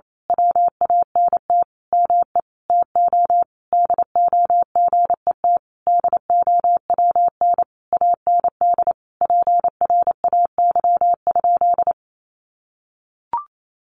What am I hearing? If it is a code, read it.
Morse code, heard as a keyed tone, about 28 wpm, 700 Hz. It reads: WANT ME TO DO GET DOWN AND PRAY?
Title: A Sound Of Thunder 28wpm